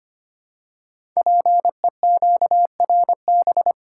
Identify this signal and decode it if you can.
Morse code, heard as a keyed tone, about 25 wpm, 700 Hz. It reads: PEQRB
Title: Code Groups - 5 Letters - 25wpm